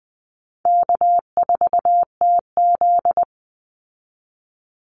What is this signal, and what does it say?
Morse code, heard as a keyed tone, about 19 wpm, 700 Hz. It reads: K4TZ
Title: Easy Call Signs - 20wpm